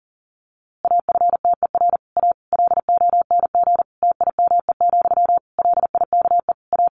Morse code, heard as a keyed tone, about 40 wpm, 700 Hz. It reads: AFTER A LONG TIME, LIKE A
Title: A Sound Of Thunder 40wpm